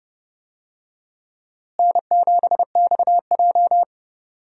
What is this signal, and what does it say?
Morse code, heard as a keyed tone, about 32 wpm, 700 Hz. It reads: N7XJ
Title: USA Call Signs - 4 Characters - Repeated 3x - 30wpm